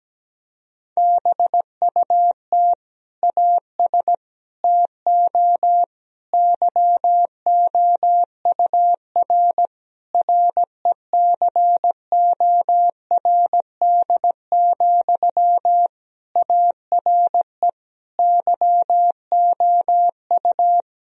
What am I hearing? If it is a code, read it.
Morse code, heard as a keyed tone, about 17 wpm, 700 Hz. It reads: BUT AS TO YOUR RECORD, ARE YOU
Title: Sentences from Top 1000 Words - Review Part 3 - 17wpm